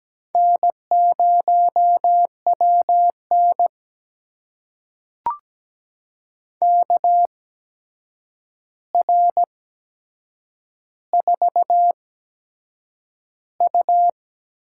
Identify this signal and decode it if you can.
Morse code, heard as a keyed tone, about 17 wpm, spaced as for 7 wpm, 700 Hz. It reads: N0WN KR4U
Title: USA Call Signs - 4 Characters - 8x Character Spacing - 17wpm